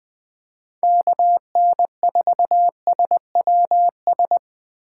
Morse code, heard as a keyed tone, about 20 wpm, 700 Hz. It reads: KN4SWS
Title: 2023 POTA Activator Call Signs - 20wpm